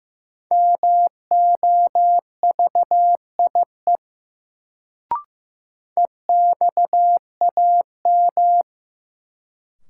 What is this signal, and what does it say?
Morse code, heard as a keyed tone, about 15 wpm, 700 Hz. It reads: MOVIE EXAM
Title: Sentences from Top 1000 Words - Review Part 5 - New Word Drill - 15wpm